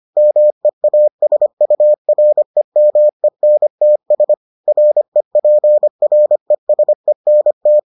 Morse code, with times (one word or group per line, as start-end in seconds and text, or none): 0.2-4.3 MEASUREMENTS
4.7-7.8 REPRESENT